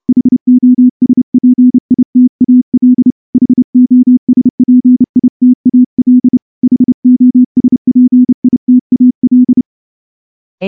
Morse code, heard as a keyed tone, about 29 wpm, 250 Hz. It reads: HOSPITAL HOSPITAL HOSPITAL
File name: morse-training-30WPM